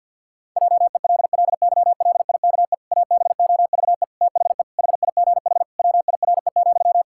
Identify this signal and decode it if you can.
Morse code, heard as a keyed tone, about 52 wpm, 700 Hz. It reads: JELLYLIKE ABOVE THE HIGH WIRE,
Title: Pangram Challenge v2 - 50wpm